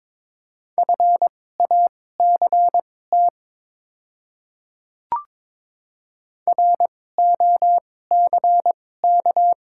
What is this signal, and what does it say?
Morse code, heard as a keyed tone, about 22 wpm, spaced as for 16 wpm, 700 Hz. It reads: FACT ROCK
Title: 2x Character Spacing - Top 500 Words - 22wpm